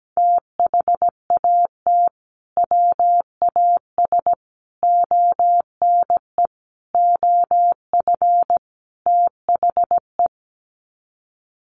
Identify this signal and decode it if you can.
Morse code, heard as a keyed tone, about 17 wpm, 700 Hz. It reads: THAT WAS ONE OF THE
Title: Sets of 5 Words 17wpm